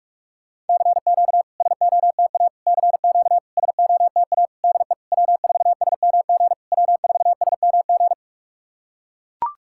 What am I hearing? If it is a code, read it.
Morse code, heard as a keyed tone, about 44 wpm, 690 Hz. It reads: CQ SOTA CQ SOTA DE W4SMG W4SMG
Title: SOTA QSO Practice 45wpm